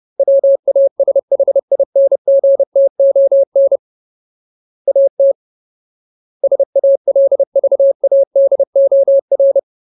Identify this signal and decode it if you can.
Morse code, heard as a keyed tone, about 30 wpm, 550 Hz. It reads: WASHINGTON AT SALVADOR
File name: Ultimate Word List / 30WPM Ultimate